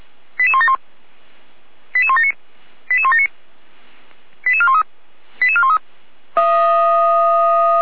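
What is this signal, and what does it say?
Five-tone selective call (ZVEI), heard as quick runs of five tones, tones of about 70 ms. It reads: 88171 88179 88179 88413 88413
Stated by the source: ZVEI five-tone selective call received on SDR.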